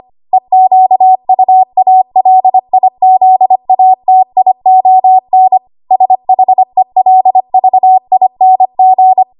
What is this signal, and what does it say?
Morse code, heard as a keyed tone, about 25 wpm, 750 Hz. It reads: EQUALIZATION SHELVING
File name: Morse / 25WPM